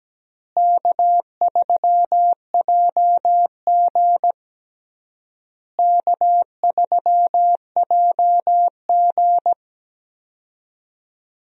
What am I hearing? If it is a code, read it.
Morse code, heard as a keyed tone, about 17 wpm, 700 Hz. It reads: K3JG K3JG
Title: USA Call Signs - 4 Characters - Repeated 3x - 17wpm